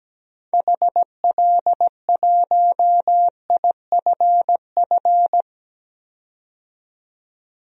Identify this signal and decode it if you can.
Morse code, heard as a keyed tone, about 17 wpm, 700 Hz. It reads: HL1IFF
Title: DXCC-100 Call Signs - 6 Characters - 17wpm